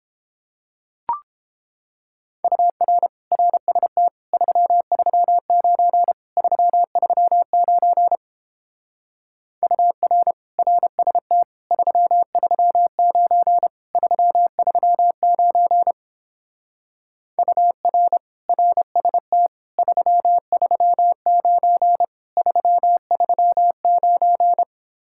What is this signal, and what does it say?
Morse code, heard as a keyed tone, about 29 wpm, 700 Hz. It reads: UR RST 339 339 UR RST 339 339 UR RST 339 339